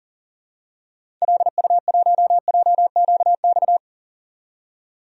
Morse code, heard as a keyed tone, about 40 wpm, 700 Hz. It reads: LU1JQX
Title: DXCC-100 Call Signs - 6 Characters - 40wpm